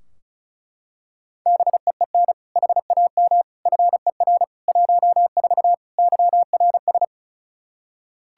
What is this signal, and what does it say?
Morse code, heard as a keyed tone, about 36 wpm, 700 Hz. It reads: BEEN HAM FER 14 YRS
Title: Practice QSO - Exchange 3 - More Info - 35wpm